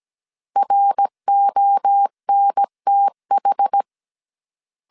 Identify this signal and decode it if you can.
Morse code, heard as a keyed tone, about 17 wpm, 800 Hz. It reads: RONTH